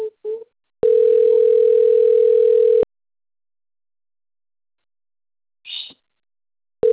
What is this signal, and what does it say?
Ringback tone (North America), 440 + 480 Hz.
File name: rg-1600-unknown- - - .86046